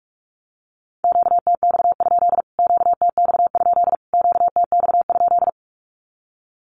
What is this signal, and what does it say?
Morse code, heard as a keyed tone, about 44 wpm, 700 Hz. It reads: QTX? QTX? QTX?
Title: Q-Codes Repeated 3x 45wpm